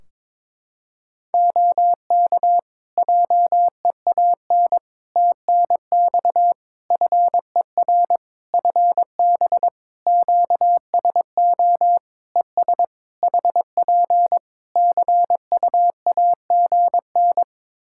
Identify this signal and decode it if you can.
Morse code, heard as a keyed tone, about 22 wpm, 700 Hz. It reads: OK JEAN TNX FER FB QSO ES HP CUAGN
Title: Practice QSO - Exchange 4 - Ending - 22wpm